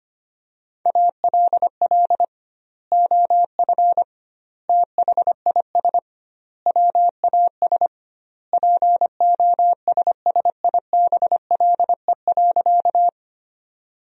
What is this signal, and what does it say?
Morse code, heard as a keyed tone, about 25 wpm, 700 Hz. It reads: ALL OF THIS WAS POSSIBLE.
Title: 2x Word-Spacing - Sentences from Top 500 Words - 25wpm